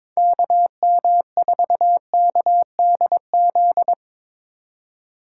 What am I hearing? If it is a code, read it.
Morse code, heard as a keyed tone, about 24 wpm, 700 Hz. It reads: KM4KDZ
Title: Speed Racing - USA 2x3 Call Signs 22wpm